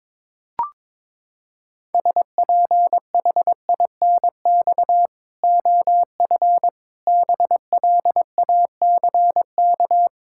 Morse code, heard as a keyed tone, about 22 wpm, 700 Hz. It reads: SPHINX OF BLACK
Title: Pangram Challenge v2 - 22wpm